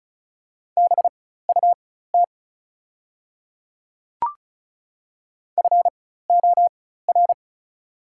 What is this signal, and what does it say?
Morse code, heard as a keyed tone, about 36 wpm, spaced as for 16 wpm, 690 Hz. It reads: BUT FOR
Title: 4x Character Spacing - Top 300 Words - 35wpm